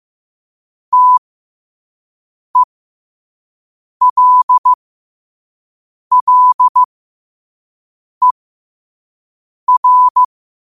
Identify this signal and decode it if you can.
Morse code, heard as a keyed tone, about 15 wpm, spaced as for 5 wpm, 1000 Hz. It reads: TELLER
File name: qotdfiles / 5wpm